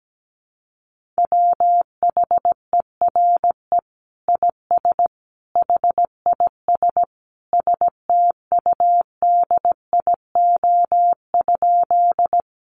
Morse code, heard as a keyed tone, about 17 wpm, 700 Hz. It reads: WHERE IS HIS STUDIO?
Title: Sentences from Top 1200 Words 17wpm